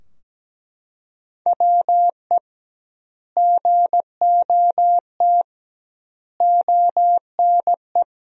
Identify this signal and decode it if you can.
Morse code, heard as a keyed tone, about 17 wpm, 700 Hz. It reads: WE GOT ONE
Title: 2x Word-Spacing - Sentences from Top 300 Words - 17wpm